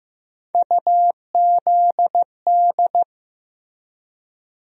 Morse code, heard as a keyed tone, about 15 wpm, 700 Hz. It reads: UZD
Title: USA Call Sign Postfixes - 3 Letters 15wpm